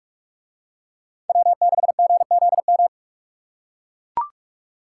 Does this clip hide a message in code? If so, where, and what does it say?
Yes: Morse code from 1.3 to 2.9 s, heard as a keyed tone, about 43 wpm, 700 Hz. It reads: W6GZM